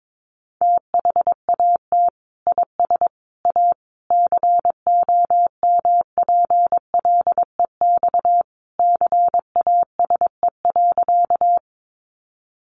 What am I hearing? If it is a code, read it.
Morse code, heard as a keyed tone, about 22 wpm, 700 Hz. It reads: THAT IS A COMPLEX CASE.
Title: Sentences from Top 900 Words 22wpm